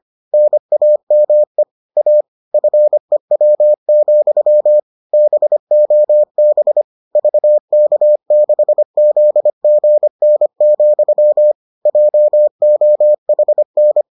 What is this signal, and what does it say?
Morse code, heard as a keyed tone, about 25 wpm, 600 Hz. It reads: NAME A FEW, BOB VK6ZGN, JOHN